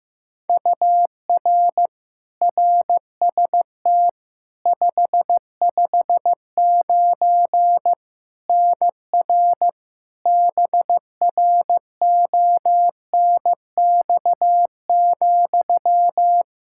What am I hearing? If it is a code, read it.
Morse code, heard as a keyed tone, about 15 wpm, 700 Hz. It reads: UR RST 559 NR BRONX,
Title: Practice QSO - Exchange 1 - Beginning - 15wpm